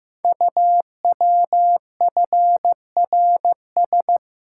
Morse code, heard as a keyed tone, about 15 wpm, 700 Hz. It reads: UWFRS
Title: Code Groups - 5 Letters - 15wpm